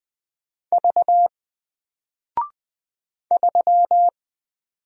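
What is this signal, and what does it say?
Morse code, heard as a keyed tone, about 20 wpm, 700 Hz. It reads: V 3
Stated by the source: Single letter and number Morse drill at 20 wpm, from N4LCD.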